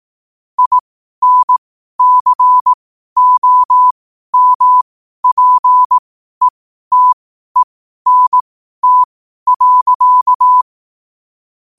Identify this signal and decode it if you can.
Morse code, heard as a keyed tone, about 18 wpm, spaced as for 12 wpm, 1000 Hz. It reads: INCOMPETENT.
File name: qotdfiles / 12wpm